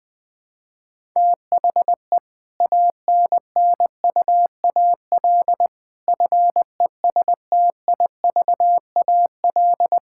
Morse code, heard as a keyed tone, about 21 wpm, 700 Hz. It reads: THE ANNUAL FESTIVAL
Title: Difficult Sentences with Hints - 20wpm